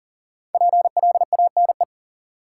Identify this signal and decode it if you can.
Morse code, heard as a keyed tone, about 40 wpm, 700 Hz. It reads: PLANE
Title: Sentences from Top 1000 Words - Review Part 3 - New Word Drill - 40wpm